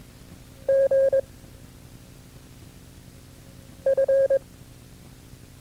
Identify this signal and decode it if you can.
Morse code, heard as a keyed tone, about 22 wpm, 540 Hz. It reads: G F
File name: FreeFiles / CW-practice / KB6MT-tape2-side1